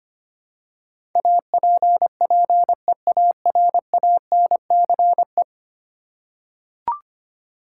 Morse code, heard as a keyed tone, about 25 wpm, 700 Hz. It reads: APPEARANCE
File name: 10 Letter Words - First 3 Letters Given - 25wpm